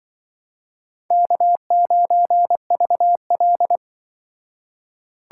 Morse code, heard as a keyed tone, about 25 wpm, 700 Hz. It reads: K9VL